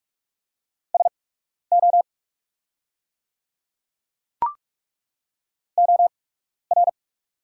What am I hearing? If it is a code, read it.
Morse code, heard as a keyed tone, about 43 wpm, spaced as for 12 wpm, 700 Hz. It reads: SO OR
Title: 8x Character Spacing - Top 100 Words - 45wpm